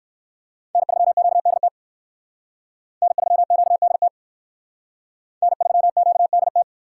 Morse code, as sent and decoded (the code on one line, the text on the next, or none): -. ...-- --.- -.. - / -. ...-- --.- -.. - / -. ...-- --.- -.. -
N3QDT N3QDT N3QDT